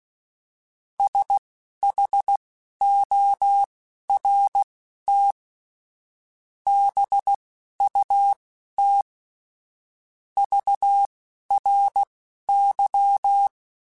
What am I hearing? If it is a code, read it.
Morse code, heard as a keyed tone, about 16 wpm, spaced as for 11 wpm, 800 Hz. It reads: SHORT BUT VRY
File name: QSOs / QSOs-mp3s / WA7QOX 2@10